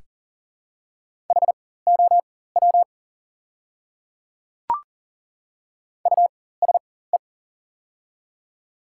Morse code, heard as a keyed tone, about 39 wpm, spaced as for 19 wpm, 690 Hz. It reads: HOW USE
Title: 4x Character Spacing - Top 100 Words - 40wpm